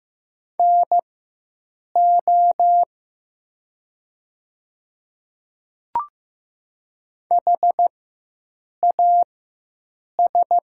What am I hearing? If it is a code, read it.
Morse code, heard as a keyed tone, about 15 wpm, spaced as for 7 wpm, 700 Hz. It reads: NO HAS
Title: 4x Character Spacing - Top 200 Words - 15wpm